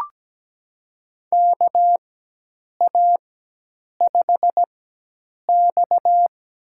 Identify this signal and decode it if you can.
Morse code, heard as a keyed tone, about 17 wpm, spaced as for 8 wpm, 700 Hz. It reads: KA5X
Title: USA Call Signs - 4 Characters - 4x Character Spacing - 17wpm